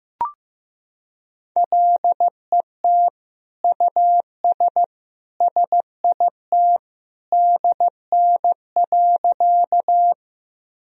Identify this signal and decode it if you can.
Morse code, heard as a keyed tone, about 15 wpm, 700 Hz. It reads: LET US SIT DN.
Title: Sentences from Top 500 Words and Abbreviations 15wpm